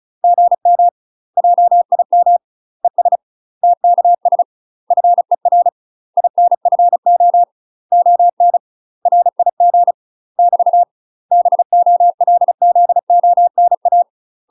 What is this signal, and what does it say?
Morse code, heard as a keyed tone, about 36 wpm, 700 Hz. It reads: GM JIM ES TKS FER INFO ON RIG <BT> BOLZONA
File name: UK MASTER QSOs TRAINING / 35WPM / UK MASTER QSOs 0013